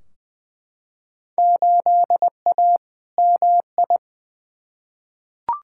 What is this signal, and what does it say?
Morse code, heard as a keyed tone, about 20 wpm, 700 Hz. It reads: 8A MI
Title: Field Day Exchanges Encore - 20wpm